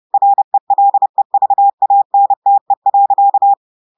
Morse code, heard as a keyed tone, about 30 wpm, 800 Hz. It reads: RELEVANTE.
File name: audios / portugais / portugais texte en clair 2 vitesse 30 mots minutes